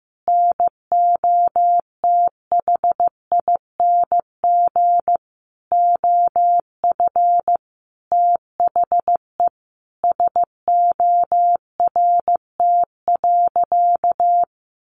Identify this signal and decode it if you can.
Morse code, heard as a keyed tone, about 15 wpm, 700 Hz. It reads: NOTHING OF THE SORT.
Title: Sentences from Top 1300 Words 15wpm